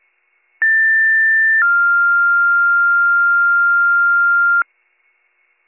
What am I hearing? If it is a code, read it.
Two-tone page, heard as a one-second tone, then a three-second tone, about 1800 then 1400 Hz, with noise. It is heard from 0.6 to 4.6 s.